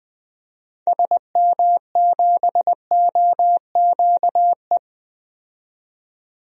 Morse code, heard as a keyed tone, about 20 wpm, 700 Hz. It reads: SM7OQE
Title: DXCC-100 Call Signs - 6 Characters - 20wpm